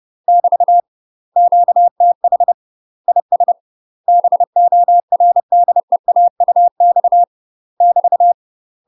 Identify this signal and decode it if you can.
Morse code, heard as a keyed tone, about 30 wpm, 700 Hz. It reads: <BT> QTH IS BORDEAUX <BT>
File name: UK MASTER QSOs TRAINING / 30WPM / UK MASTER QSOs 0030